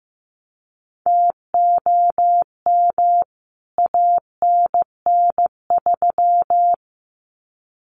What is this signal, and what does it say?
Morse code, heard as a keyed tone, about 15 wpm, 700 Hz. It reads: TOM ANN3
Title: CWT Exchange 15wpm